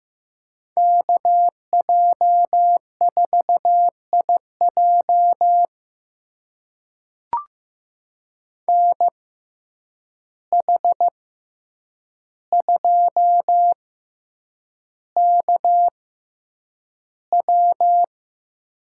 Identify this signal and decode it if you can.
Morse code, heard as a keyed tone, about 15 wpm, spaced as for 8 wpm, 700 Hz. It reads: KJ4IJ NH2KW
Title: USA Call Signs - 5 Characters - 6x Character Spacing - 15wpm